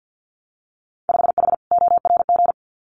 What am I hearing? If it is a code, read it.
Morse code, heard as a keyed tone, about 50 wpm, 700 Hz. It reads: 5H ORG
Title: Winter Field Day Exchanges Encore 50wpm